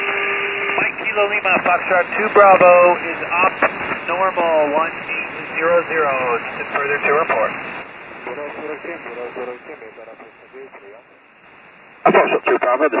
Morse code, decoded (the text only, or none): none